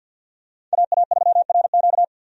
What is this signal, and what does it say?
Morse code, heard as a keyed tone, about 49 wpm, 700 Hz. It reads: AA3RQ
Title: USA Call Signs - 5 Characters - 50wpm